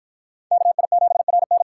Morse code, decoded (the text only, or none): KI7RN